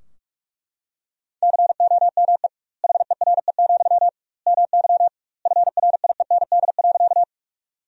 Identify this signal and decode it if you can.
Morse code, heard as a keyed tone, about 43 wpm, 700 Hz. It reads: COME HERE, MY FRIEND.
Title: 2x Word-Spacing - Sentences from Top 300 Words - 45wpm